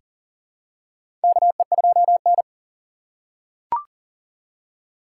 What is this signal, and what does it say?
Morse code, heard as a keyed tone, about 40 wpm, 700 Hz. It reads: KE2N
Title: USA Call Signs - 4 Characters - 6x Character Spacing - 40wpm